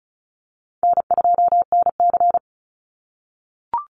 Morse code, heard as a keyed tone, about 34 wpm, 700 Hz. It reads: N2NC
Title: Call Signs - 35wpm